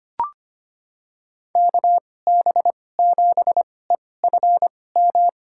Morse code, heard as a keyed tone, about 25 wpm, spaced as for 18 wpm, 700 Hz. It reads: KB7EFM
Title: USA Call Signs - 6 Characters - 2x Character Spacing - 25wpm